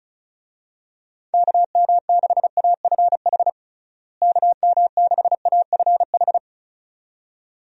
Morse code, heard as a keyed tone, about 36 wpm, 700 Hz. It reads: KM6AFH KM6AFH
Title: USA Call Signs - 6 Characters - Repeated 2x - 35wpm